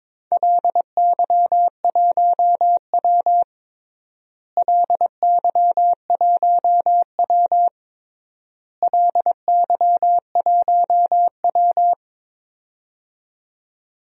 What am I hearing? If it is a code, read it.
Morse code, heard as a keyed tone, about 22 wpm, 700 Hz. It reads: LY1W LY1W LY1W
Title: DXCC-100 Call Signs - 4 Characters - Repeated 3x - 22wpm